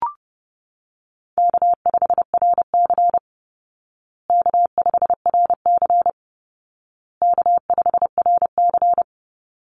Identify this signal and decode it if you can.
Morse code, heard as a keyed tone, about 30 wpm, 700 Hz. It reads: K5RC K5RC K5RC
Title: Call Signs Repeated 2x - 30wpm